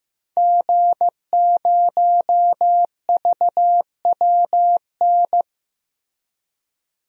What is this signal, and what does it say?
Morse code, heard as a keyed tone, about 15 wpm, 700 Hz. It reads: G0VWN
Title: DXCC-100 Call Signs - 5 Characters - Repeated 2x - 15wpm